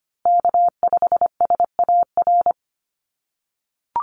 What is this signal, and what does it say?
Morse code, heard as a keyed tone, about 25 wpm, 700 Hz. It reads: K5SAR
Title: USA Call Signs Repeated 3x 25wpm